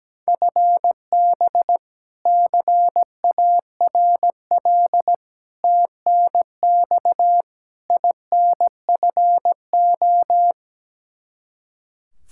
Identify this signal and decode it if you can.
Morse code, heard as a keyed tone, about 17 wpm, 700 Hz. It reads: FB CARL TNX INFO
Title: Practice QSO - Exchange 3 - More Info - 17wpm